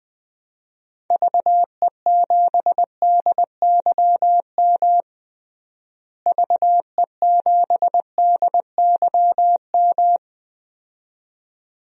Morse code, heard as a keyed tone, about 20 wpm, 700 Hz. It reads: VE7DYM VE7DYM
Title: DXCC-100 Call Signs - 6 Characters - Repeated 2x - 20wpm